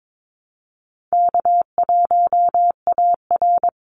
Morse code, heard as a keyed tone, about 22 wpm, 700 Hz. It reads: K1AR
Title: Easy Call Signs Repeated 2x - 22wpm